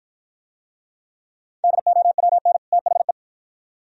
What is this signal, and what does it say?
Morse code, heard as a keyed tone, about 53 wpm, 690 Hz. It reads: DOWN THE